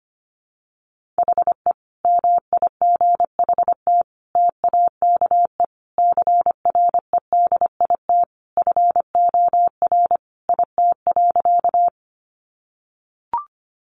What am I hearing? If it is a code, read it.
Morse code, heard as a keyed tone, about 25 wpm, 700 Hz. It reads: HE MIGHT TAKE CREDIT FOR IT.
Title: Sentences from Top 1000 Words 25wpm